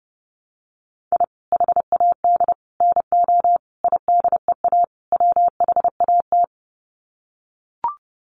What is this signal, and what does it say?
Morse code, heard as a keyed tone, about 30 wpm, 700 Hz. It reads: I HAD NO IDEA WHAT
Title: Sets of 5 Words Encore 30wpm